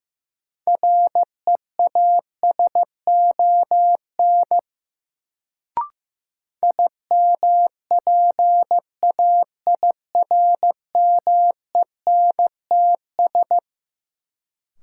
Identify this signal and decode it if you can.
Morse code, heard as a keyed tone, about 15 wpm, 700 Hz. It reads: REASON IMPAIRMENTS
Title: Sets of 5 Words Encore - New Word Drill - 15wpm